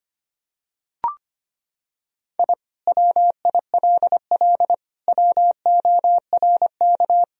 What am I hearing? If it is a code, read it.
Morse code, heard as a keyed tone, about 25 wpm, 700 Hz. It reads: I WILL WORK
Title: Sentences from Top 1000 Words - Review Part 4 - 25wpm